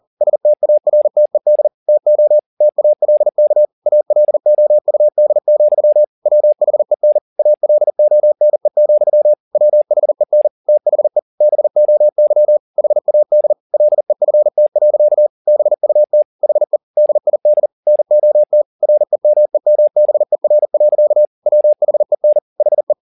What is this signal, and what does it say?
Morse code, heard as a keyed tone, about 40 wpm, 600 Hz. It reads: STARTED TO TALK ALOUD, WHEN ALONE, WHEN THE BOY HAD LEFT. BUT HE DID NOT REMEMBER. WHEN HE